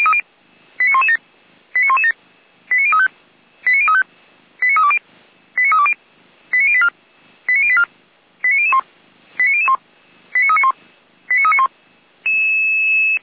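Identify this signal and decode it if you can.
Five-tone selective call (ZVEI), heard as quick runs of five tones, tones of about 80 ms. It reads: 89117 89117 89035 89035 89320 89320 89084 89084 89001 89001 89391 89391
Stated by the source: ZVEI five-tone call received by SDR.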